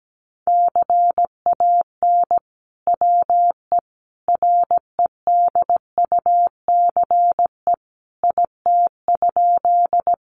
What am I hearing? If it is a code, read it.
Morse code, heard as a keyed tone, about 17 wpm, 700 Hz. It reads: CAN WE REDUCE IT?
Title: Sentences from Top 1100 Words 17wpm